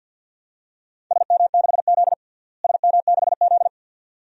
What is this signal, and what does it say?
Morse code, heard as a keyed tone, about 51 wpm, 700 Hz. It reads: SM6Z SM6Z
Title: DXCC-100 Call Signs - 4 Characters - Repeated 2x - 50wpm